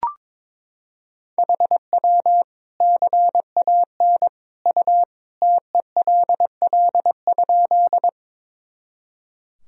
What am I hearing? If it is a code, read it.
Morse code, heard as a keyed tone, about 22 wpm, 700 Hz. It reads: HW CAN U TELL?
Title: Sentences from Top 200 Words and Abbreviations 22wpm